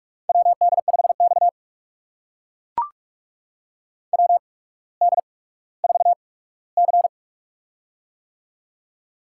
Morse code, heard as a keyed tone, about 44 wpm, spaced as for 19 wpm, 690 Hz. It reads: WD5X WD4C
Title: USA Call Signs - 4 Characters - 8x Character Spacing - 45wpm